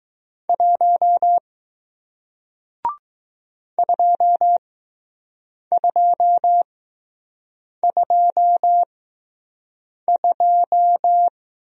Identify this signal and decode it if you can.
Morse code, heard as a keyed tone, about 19 wpm, spaced as for 6 wpm, 700 Hz. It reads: 1 2222